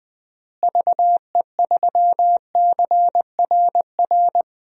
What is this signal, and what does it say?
Morse code, heard as a keyed tone, about 20 wpm, 700 Hz. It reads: VE3CRR
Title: DXCC-100 Call Signs - 6 Characters - Repeated 3x - 20wpm